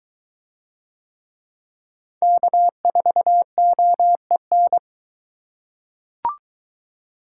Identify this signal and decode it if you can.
Morse code, heard as a keyed tone, about 22 wpm, 700 Hz. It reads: K4OEN